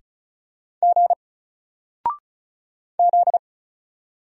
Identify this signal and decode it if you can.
Morse code, heard as a keyed tone, about 36 wpm, 700 Hz. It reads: G Z
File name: Single-Letter-Number-35wpm